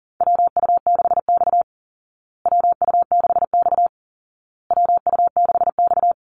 Morse code, heard as a keyed tone, about 40 wpm, 700 Hz. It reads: WU6X WU6X WU6X
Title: Call Signs Repeated 2x - 40wpm